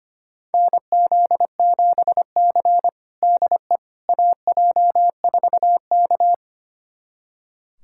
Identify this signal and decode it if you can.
Morse code, heard as a keyed tone, about 25 wpm, 700 Hz. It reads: NZ7C DE AJ4K
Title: USA Callsign DE Callsign - 4 Characters - 25wpm